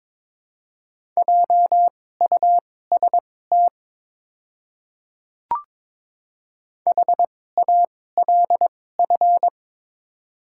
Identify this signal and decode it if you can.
Morse code, heard as a keyed tone, about 22 wpm, spaced as for 16 wpm, 700 Hz. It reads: JUST HALF
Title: 2x Character Spacing - Top 500 Words - 22wpm